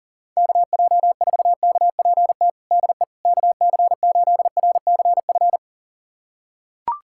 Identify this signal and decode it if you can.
Morse code, heard as a keyed tone, about 40 wpm, 700 Hz. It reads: KJ4KPT DE KC8RCF
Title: USA Callsign DE Callsign - 6 Characters - 40wpm